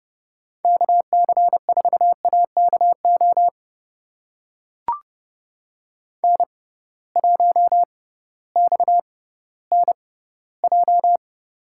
Morse code, heard as a keyed tone, about 30 wpm, spaced as for 16 wpm, 700 Hz. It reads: KC4AKO N1XNJ
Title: USA Call Signs - 6x Character Spacing - 30wpm